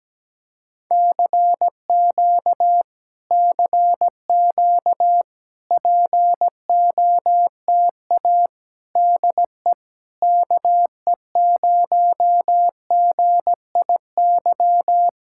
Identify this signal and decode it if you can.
Morse code, heard as a keyed tone, about 17 wpm, 700 Hz. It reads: CQ CQ POTA DE KE0GIY
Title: Speed-Racing Common QSO Phrases - 17wpm